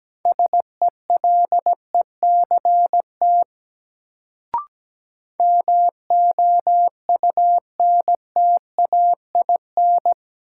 Morse code, heard as a keyed tone, about 17 wpm, 700 Hz. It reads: SELECT MOUNTAIN
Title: Sentences from Top 1300 Words - New Word Drill Alt - 17wpm